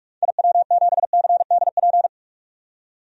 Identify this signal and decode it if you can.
Morse code, heard as a keyed tone, about 47 wpm, 700 Hz. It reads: IW7CDP
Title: DXCC-100 Call Signs - 6 Characters - 45wpm